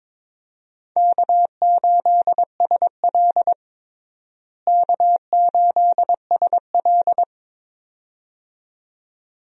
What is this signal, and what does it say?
Morse code, heard as a keyed tone, about 22 wpm, 700 Hz. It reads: K8SL K8SL
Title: USA Call Signs - 4 Characters - Repeated 2x - 22wpm